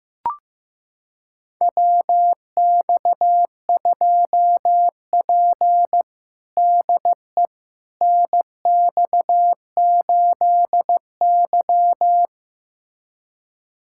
Morse code, heard as a keyed tone, about 15 wpm, 700 Hz. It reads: WX2P DE NX8Y
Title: USA Callsign DE Callsign - 4 Characters - 15wpm